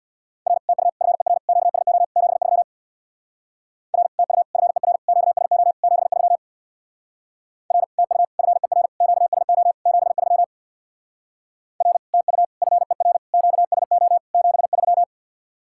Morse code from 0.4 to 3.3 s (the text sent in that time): R TU FER QSO 73